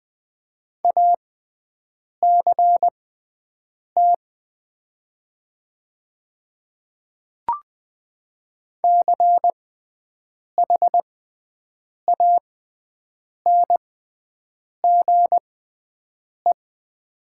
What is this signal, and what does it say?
Morse code, heard as a keyed tone, about 20 wpm, spaced as for 7 wpm, 700 Hz. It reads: ACT CHANGE
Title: 6x Character Spacing - Top 200 Words - 20wpm